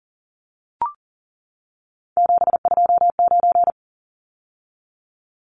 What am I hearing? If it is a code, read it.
Morse code, heard as a keyed tone, about 40 wpm, 700 Hz. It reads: 729
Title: Numbers - 3-Digit 40wpm